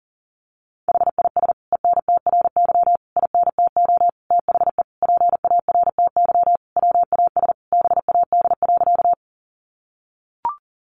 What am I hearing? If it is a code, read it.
Morse code, heard as a keyed tone, about 40 wpm, 700 Hz. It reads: HIS ENTRY INTO THE PARTY WAS BAD.
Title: Sentences from Top 1500 Words 40wpm